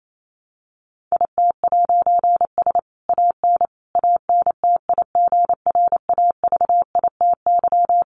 Morse code, heard as a keyed tone, about 28 wpm, 700 Hz. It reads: IT'S AN ANTIGRAVITY
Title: A Sound Of Thunder 28wpm